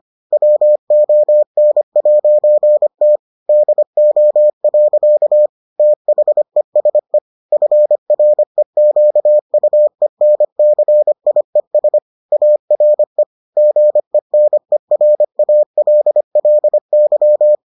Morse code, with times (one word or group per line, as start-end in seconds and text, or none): 0.3-3.2 WON'T
3.5-5.5 DO.
5.8-7.2 THESE
7.5-12.0 FREQUENCIES
12.3-13.2 ARE
13.6-17.6 GENERALLY